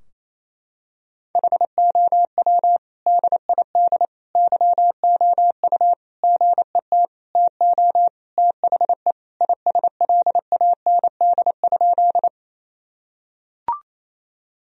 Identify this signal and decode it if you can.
Morse code, heard as a keyed tone, about 27 wpm, 700 Hz. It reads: HOW DID YOU GET TO THE ISLAND?
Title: Sentences from Top 1000 Words - Review Part 3 - 28wpm